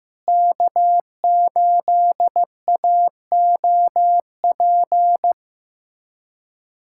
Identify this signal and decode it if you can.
Morse code, heard as a keyed tone, about 15 wpm, 700 Hz. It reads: K8AOP
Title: USA Callsign DE Callsign - 5 Characters - 15wpm